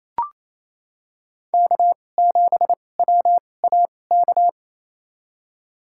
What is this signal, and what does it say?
Morse code, heard as a keyed tone, about 28 wpm, spaced as for 20 wpm, 700 Hz. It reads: K7WAK
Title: USA Call Signs - 5 Characters - 2x Character Spacing - 28wpm